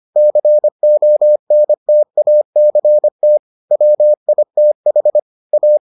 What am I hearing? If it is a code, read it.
Morse code, heard as a keyed tone, about 25 wpm, 600 Hz. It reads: CONTACT WITH A